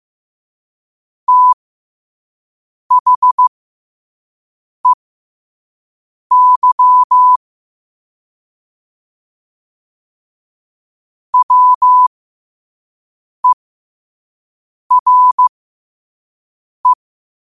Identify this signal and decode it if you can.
Morse code, heard as a keyed tone, about 15 wpm, spaced as for 5 wpm, 1000 Hz. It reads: THEY WERE
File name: qotdfiles / 5wpm